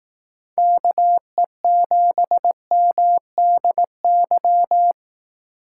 Morse code, heard as a keyed tone, about 18 wpm, 700 Hz. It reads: KE7MDY
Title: Speed Racing - USA 2x3 Call Signs 15wpm